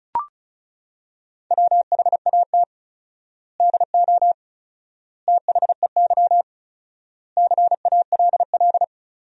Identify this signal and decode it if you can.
Morse code, heard as a keyed tone, about 34 wpm, 700 Hz. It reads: WHAT DO THEY CALL
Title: 4x Word-Spacing - Sentences from Top 100 Words - 35wpm